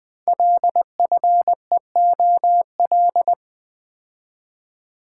Morse code, heard as a keyed tone, about 20 wpm, 700 Hz. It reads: LFEOL
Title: Code Groups - 5 Letters - 20wpm